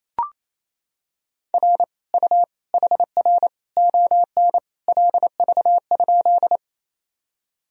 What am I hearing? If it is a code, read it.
Morse code, heard as a keyed tone, about 27 wpm, 700 Hz. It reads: R U HR ON LV?
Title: Sentences from Top 400 Words and Abbreviations 28wpm